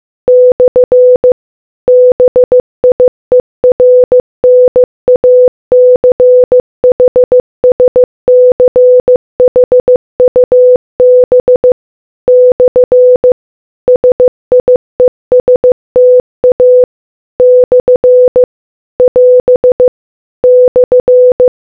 Morse code, heard as a keyed tone, about 15 wpm, 500 Hz. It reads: / BIERNACHSCHUB / SIESTA / <AS> /